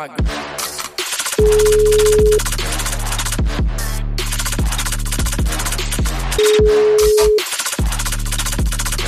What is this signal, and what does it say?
Ringback tone (Europe, 425 Hz), with noise.